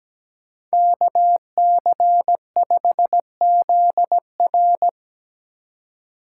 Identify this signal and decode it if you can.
Morse code, heard as a keyed tone, about 17 wpm, 700 Hz. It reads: KC5ZR
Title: USA Call Signs - 2x Character Spacing - 17wpm